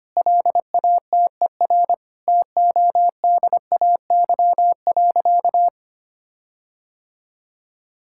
Morse code, heard as a keyed tone, about 25 wpm, 700 Hz. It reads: LATER TODAY.
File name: practice / 25wpm / Sentences with 4-Word Phrases - 25wpm